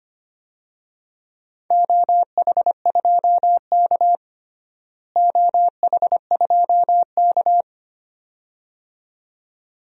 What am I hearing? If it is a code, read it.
Morse code, heard as a keyed tone, about 25 wpm, 700 Hz. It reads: OH2K OH2K
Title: DXCC-100 Call Signs - 4 Characters - Repeated 2x - 25wpm